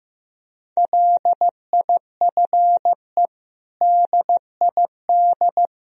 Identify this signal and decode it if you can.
Morse code, heard as a keyed tone, about 15 wpm, 700 Hz. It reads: LIFE DID
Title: Sentences from Top 1000 Words - Review Part 1 - 15wpm